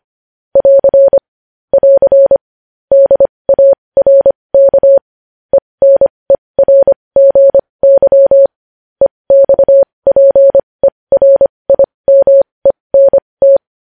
Morse code, heard as a keyed tone, about 25 wpm, spaced as for 20 wpm, 550 Hz. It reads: <AR> <AR> DARK ENERGY EXPERIMENT
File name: podcast / News headlines-1x-25x20